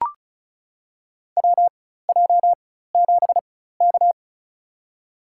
Morse code, heard as a keyed tone, about 36 wpm, spaced as for 16 wpm, 700 Hz. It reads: WJ7K
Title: USA Call Signs - 4 Characters - 4x Character Spacing - 35wpm